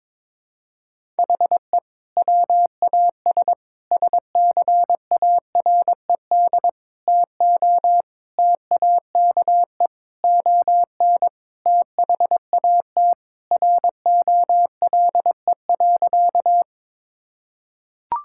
Morse code, heard as a keyed tone, about 22 wpm, 700 Hz. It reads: HE WAS SCARED TO TAKE ON THAT ROLE.
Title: Sentences from Top 1000 Words - Review Part 6 - 22wpm